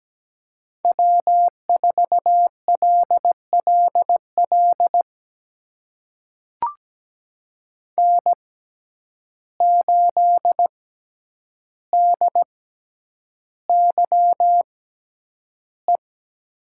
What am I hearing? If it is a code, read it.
Morse code, heard as a keyed tone, about 17 wpm, spaced as for 9 wpm, 700 Hz. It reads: W4LLL N8DYE
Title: USA Call Signs - 5 Characters - 6x Character Spacing - 17wpm